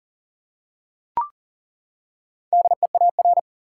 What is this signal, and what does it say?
Morse code, heard as a keyed tone, about 40 wpm, 700 Hz. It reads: DEAR
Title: Four Letter Words - 40wpm